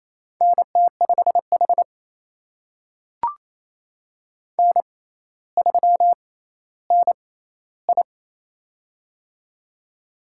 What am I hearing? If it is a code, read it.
Morse code, heard as a keyed tone, about 28 wpm, spaced as for 14 wpm, 700 Hz. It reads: NT5H N3NI
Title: USA Call Signs - 4 Characters - 6x Character Spacing - 28wpm